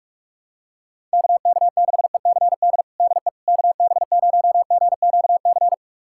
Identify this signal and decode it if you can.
Morse code, heard as a keyed tone, about 44 wpm, 700 Hz. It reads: KK6ECD DE KB0GQC
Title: USA Callsign DE Callsign - 6 Characters - 45wpm